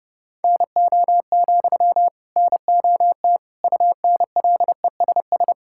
Morse code, heard as a keyed tone, about 30 wpm, 700 Hz. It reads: NO, NOT UNLESS
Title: Sentences from Top 1000 Words - Review Part 6 - 30wpm